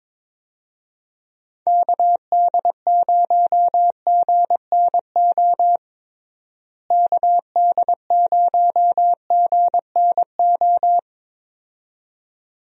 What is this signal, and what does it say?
Morse code, heard as a keyed tone, about 22 wpm, 700 Hz. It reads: KD0GNO KD0GNO
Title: USA Call Signs - 6 Characters - Repeated 2x - 22wpm